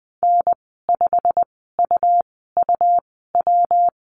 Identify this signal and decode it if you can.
Morse code, heard as a keyed tone, about 20 wpm, spaced as for 15 wpm, 700 Hz. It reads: N5UUW
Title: 140 Lesson 23g - Letter f - Call Signs 20wpm